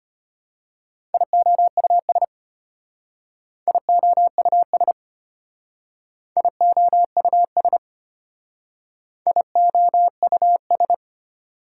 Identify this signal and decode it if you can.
Morse code, heard as a keyed tone, about 31 wpm, 690 Hz. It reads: IOUS IOUS IOUS IOUS